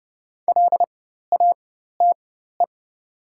Morse code, heard as a keyed tone, about 31 wpm, spaced as for 14 wpm, 690 Hz. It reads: LATE
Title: 4x Character Spacing - Top 300 Words - 30wpm